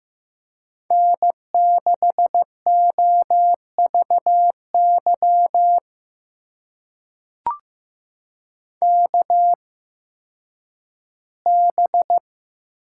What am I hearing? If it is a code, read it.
Morse code, heard as a keyed tone, about 15 wpm, spaced as for 10 wpm, 700 Hz. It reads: N6OVY KB
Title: USA Call Signs - 8x Character Spacing - 15wpm